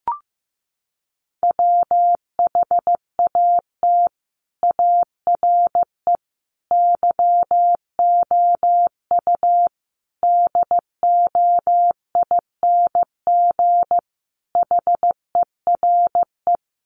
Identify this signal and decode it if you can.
Morse code, heard as a keyed tone, about 15 wpm, 700 Hz. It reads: WHAT ARE YOU DOING HERE
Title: Sets of 5 Words 15wpm